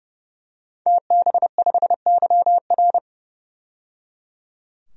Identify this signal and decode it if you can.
Morse code, heard as a keyed tone, about 30 wpm, 700 Hz. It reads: TB5YR
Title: DXCC-100 Call Signs - 5 Characters - 30wpm